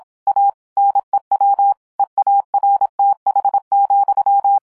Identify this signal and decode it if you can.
Morse code, heard as a keyed tone, about 26 wpm, 800 Hz. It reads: A NEW EARTH,